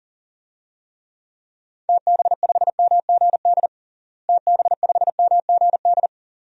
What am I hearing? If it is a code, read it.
Morse code, heard as a keyed tone, about 40 wpm, 700 Hz. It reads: TB5MGD TB5MGD
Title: DXCC-100 Call Signs - 6 Characters - Repeated 2x - 40wpm